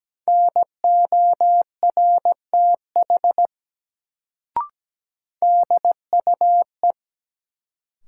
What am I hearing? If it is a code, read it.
Morse code, heard as a keyed tone, about 17 wpm, 700 Hz. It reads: NORTH DUE
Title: Sets of 5 Words Encore - New Word Drill - 17wpm